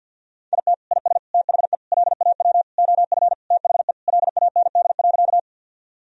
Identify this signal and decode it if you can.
Morse code, heard as a keyed tone, about 52 wpm, 700 Hz. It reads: IT IS THE LAW OF THE LAND.
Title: Sentences from Top 1000 Words - Review Part 3 - 50wpm